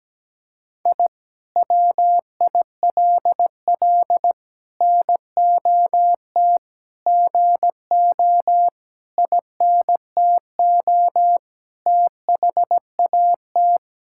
Morse code, heard as a keyed tone, about 17 wpm, 700 Hz. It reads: I WILL NOT GO INTO THAT
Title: Sentences from Top 1000 Words - Review Part 6 - 17wpm